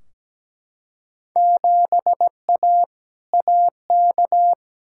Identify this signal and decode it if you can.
Morse code, heard as a keyed tone, about 17 wpm, 700 Hz. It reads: 7A AK
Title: Field Day Exchanges Encore - 17wpm